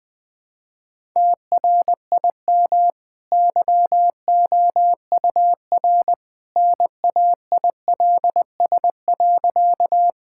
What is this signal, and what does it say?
Morse code, heard as a keyed tone, about 20 wpm, 700 Hz. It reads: TRIM YOUR NAILS.
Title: Easy Sentences with Hints - 20wpm